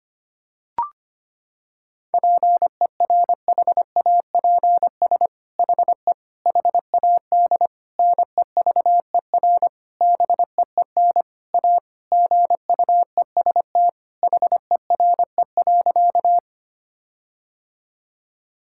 Morse code, heard as a keyed tone, about 25 wpm, 700 Hz. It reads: PERHAPS HE HAD NEVER BEEN A GUEST HERE.
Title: Sentences from Top 1600 Words 25wpm